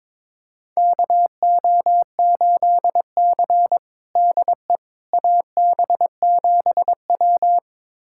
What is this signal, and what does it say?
Morse code, heard as a keyed tone, about 21 wpm, 700 Hz. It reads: KO8C DE AB7W
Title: USA Callsign DE Callsign - 4 Characters - 22wpm